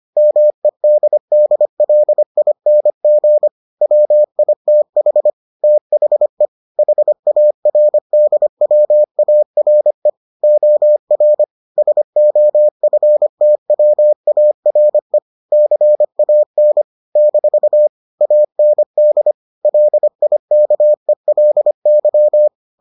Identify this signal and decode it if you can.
Morse code, heard as a keyed tone, about 25 wpm, 600 Hz. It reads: MEDDLING WITH THE HARDWARE OR SOFTWARE CAN - AND LIKELY